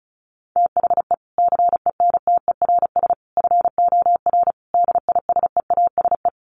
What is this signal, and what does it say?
Morse code, heard as a keyed tone, about 34 wpm, 700 Hz. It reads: THE CENTERS FOR DISEASE
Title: Sets of 5 Words Encore2 35wpm